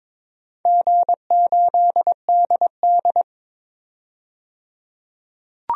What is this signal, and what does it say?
Morse code, heard as a keyed tone, about 22 wpm, 700 Hz. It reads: G8DD
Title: DXCC-100 Call Signs - 4 Characters - 22wpm